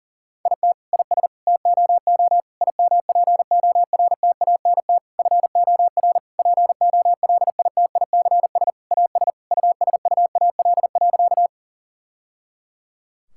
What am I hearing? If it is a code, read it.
Morse code, heard as a keyed tone, about 40 wpm, 700 Hz. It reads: IT IS TOO IMPORTANT FOR POLITICS AS USUAL.
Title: Sentences from Top 1000 Words - Review Part 6 - 40wpm